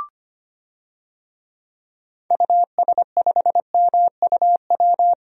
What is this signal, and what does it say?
Morse code, heard as a keyed tone, about 25 wpm, 700 Hz. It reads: US5MUW
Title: DXCC-100 Call Signs - 6 Characters - Repeated 3x - 25wpm